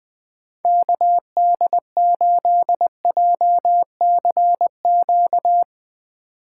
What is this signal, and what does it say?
Morse code, heard as a keyed tone, about 20 wpm, 700 Hz. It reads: KD8JCQ
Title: USA Call Signs - 6 Characters - 3x Character Spacing - 20wpm